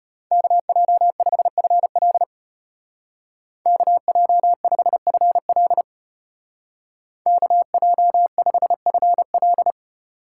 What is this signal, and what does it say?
Morse code, heard as a keyed tone, about 32 wpm, 700 Hz. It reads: KJ5FL KJ5FL KJ5FL